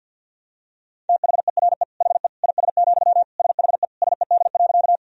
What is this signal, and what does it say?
Morse code, heard as a keyed tone, about 51 wpm, 700 Hz. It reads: THERE HE IS, SHE SED.
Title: Sentences from Top 100 Words and Abbreviations 50wpm